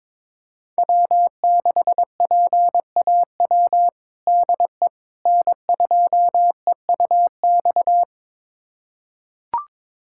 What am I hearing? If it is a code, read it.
Morse code, heard as a keyed tone, about 22 wpm, 700 Hz. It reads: W6PAW DE N2EUX
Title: USA Callsign DE Callsign - 5 Characters - 22wpm